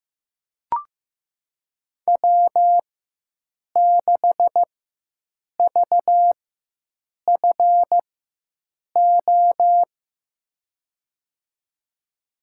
Morse code, heard as a keyed tone, about 15 wpm, spaced as for 7 wpm, 700 Hz. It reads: W6VFO
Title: USA Call Signs - 5 Characters - 4x Character Spacing - 15wpm